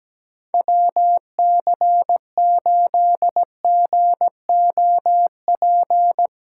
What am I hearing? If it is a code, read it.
Morse code, heard as a keyed tone, about 17 wpm, 700 Hz. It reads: WC8GOP
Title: USA Callsign DE Callsign 17wpm